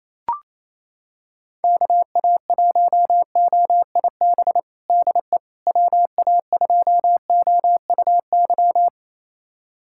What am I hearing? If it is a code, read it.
Morse code, heard as a keyed tone, about 28 wpm, 700 Hz. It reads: KA1OIB DE WA2OUY
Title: USA Callsign DE Callsign 28wpm